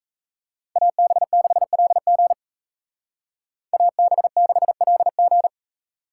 Morse code, heard as a keyed tone, about 34 wpm, 700 Hz. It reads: AB6LG AB6LG
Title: Speed Racing - USA 2x2 Call Signs 28wpm